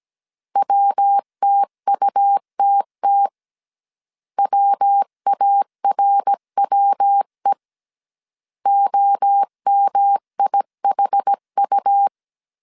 Morse code, heard as a keyed tone, about 17 wpm, 800 Hz. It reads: WTUTT WARWE OMIHU
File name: CW / DET / 1 3 1500C S C 16wpm 2148